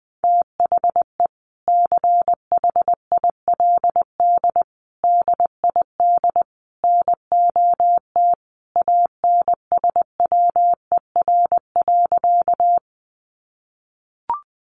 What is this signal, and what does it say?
Morse code, heard as a keyed tone, about 20 wpm, 700 Hz. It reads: THE CHILD DID NOT ANSWER.
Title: Sentences from Top 800 Words 20wpm